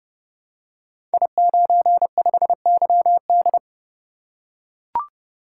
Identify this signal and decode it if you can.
Morse code, heard as a keyed tone, about 30 wpm, 700 Hz. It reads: I95YD
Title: Code Groups - 5 Letter:Numbers - 30wpm